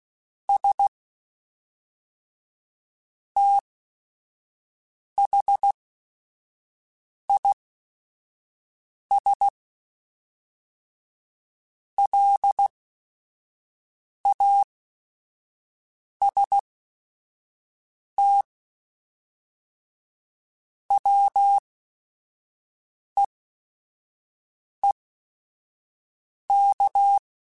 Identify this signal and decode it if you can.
Morse code, heard as a keyed tone, about 16 wpm, spaced as for 5 wpm, 800 Hz. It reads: S THIS LAST WEEK